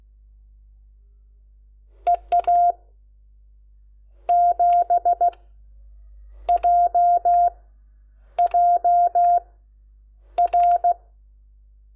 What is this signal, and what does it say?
Morse code, heard as a keyed tone, about 16 wpm, spaced as for 7 wpm, 700 Hz. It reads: EA7JJR
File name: Lecciones / CLASE Nº10